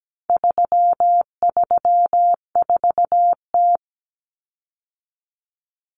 Morse code, heard as a keyed tone, about 17 wpm, 700 Hz. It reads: 334T
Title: Common Cut Numbers - 4-Digit 17wpm